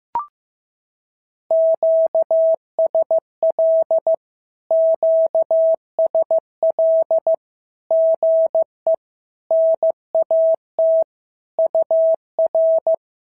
Morse code, heard as a keyed tone, about 16 wpm, 650 Hz. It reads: QSL QSL GE NAT UR
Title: SOTA QSO Practice 15wpm